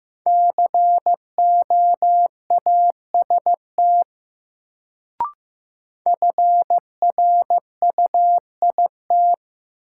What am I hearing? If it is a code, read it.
Morse code, heard as a keyed tone, about 15 wpm, 700 Hz. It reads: COAST FRUIT
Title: Sentences from Top 1300 Words - New Word Drill Alt - 15wpm